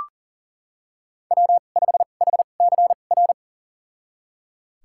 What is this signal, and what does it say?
Morse code, heard as a keyed tone, about 40 wpm, spaced as for 29 wpm, 700 Hz. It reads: W5HCR
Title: USA Call Signs - 2x Character Spacing - 40wpm